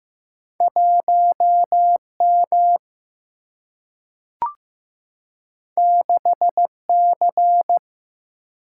Morse code, heard as a keyed tone, about 15 wpm, 700 Hz. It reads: 1 M 6 C
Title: Single-Letter Single-Number 15wpm